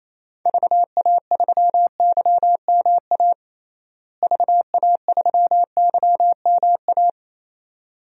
Morse code, heard as a keyed tone, about 28 wpm, 700 Hz. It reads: VA3YMA VA3YMA
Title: DXCC-100 Call Signs - 6 Characters - Repeated 2x - 28wpm